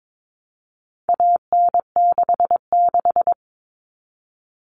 Morse code, heard as a keyed tone, about 22 wpm, 700 Hz. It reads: AN66
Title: All Cut Numbers - 4-Digit 22wpm